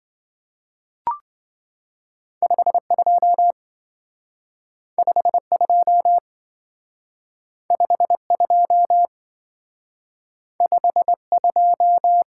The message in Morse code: 52 52 52 52

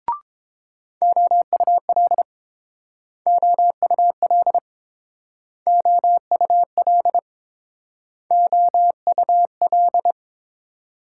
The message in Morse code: OUL OUL OUL OUL

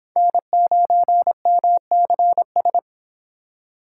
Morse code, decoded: N9MCS

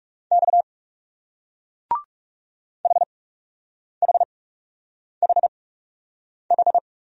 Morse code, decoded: X HHHH

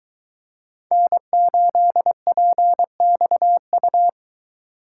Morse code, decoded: N8PXU

